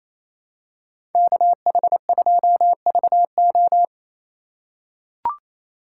Morse code, heard as a keyed tone, about 28 wpm, 700 Hz. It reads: KH2VO
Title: USA Call Signs - 5 Characters - 2x Character Spacing - 28wpm